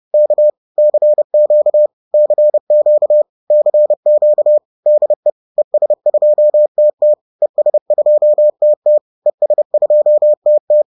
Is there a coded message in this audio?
Yes: Morse code from 0.1 to 10.8 s, heard as a keyed tone, about 30 wpm, 600 Hz. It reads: K CQ CQ CQ DE ES2TT ES2TT ES2TT